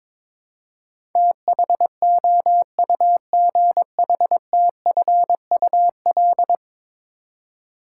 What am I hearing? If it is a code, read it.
Morse code, heard as a keyed tone, about 22 wpm, 700 Hz. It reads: THOUGHTFUL
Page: morse-code-ninja-practice-22wpm